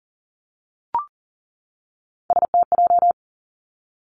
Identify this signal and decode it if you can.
Morse code, heard as a keyed tone, about 39 wpm, 700 Hz. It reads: STJ
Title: Three Challenging Letter-Number Combinations 40wpm